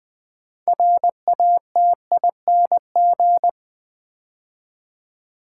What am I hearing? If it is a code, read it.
Morse code, heard as a keyed tone, about 20 wpm, 700 Hz. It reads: RATING
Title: 6 Letter Words - First 3 Letters Given - 20wpm